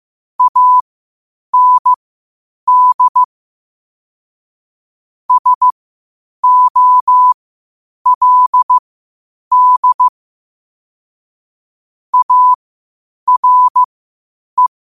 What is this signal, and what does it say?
Morse code, heard as a keyed tone, about 15 wpm, spaced as for 8 wpm, 1000 Hz. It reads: AND SOLD ARE
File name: qotdfiles / 8wpm